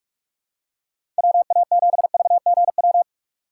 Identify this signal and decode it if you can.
Morse code, heard as a keyed tone, about 44 wpm, 700 Hz. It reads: WA7VGW
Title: USA Call Signs - 6 Characters - 6x Character Spacing - 45wpm